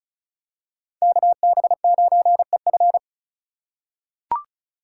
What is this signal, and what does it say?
Morse code, heard as a keyed tone, about 36 wpm, 700 Hz. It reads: KB9EF
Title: USA Call Signs - 3x Character Spacing - 35wpm